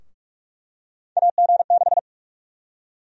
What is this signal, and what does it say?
Morse code, heard as a keyed tone, about 44 wpm, 700 Hz. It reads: AG6
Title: USA Call Sign Prefixes - 45wpm